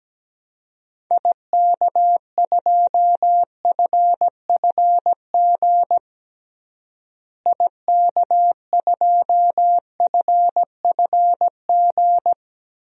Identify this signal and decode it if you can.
Morse code, heard as a keyed tone, about 17 wpm, 700 Hz. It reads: IK2FFG IK2FFG
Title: DXCC-100 Call Signs - 6 Characters - Repeated 2x - 17wpm